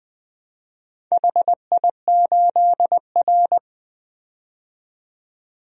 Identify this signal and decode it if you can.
Morse code, heard as a keyed tone, about 20 wpm, 700 Hz. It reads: HI8R
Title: DXCC-100 Call Signs - 4 Characters - 20wpm